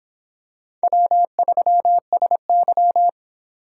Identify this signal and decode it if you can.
Morse code, heard as a keyed tone, about 26 wpm, 700 Hz. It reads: W3SY